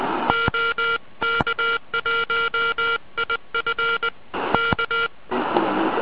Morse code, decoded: OK1IF K